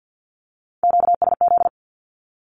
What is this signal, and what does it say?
Morse code, heard as a keyed tone, about 50 wpm, 700 Hz. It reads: QSZ
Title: Q-Codes Repeated 3x 50wpm